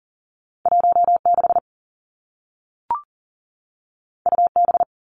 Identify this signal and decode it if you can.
Morse code, heard as a keyed tone, about 42 wpm, 700 Hz. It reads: 1 6 U B
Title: Two Challenging Letter-Number Combinations 40wpm